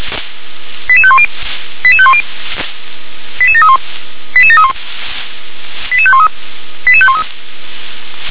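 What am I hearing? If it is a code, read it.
Five-tone selective call (ZVEI), heard as quick runs of five tones, tones of about 70 ms. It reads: 88510 88510 88731 88731 88513 88513
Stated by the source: Five-tone ZVEI selective call received on an SDR.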